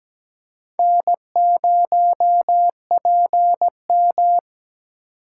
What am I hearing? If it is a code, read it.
Morse code, heard as a keyed tone, about 17 wpm, 700 Hz. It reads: N0PM
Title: USA Call Signs - 4 Characters - 4x Character Spacing - 17wpm